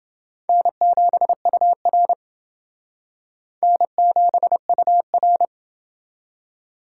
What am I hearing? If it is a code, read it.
Morse code, heard as a keyed tone, about 28 wpm, 700 Hz. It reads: N7UR N7UR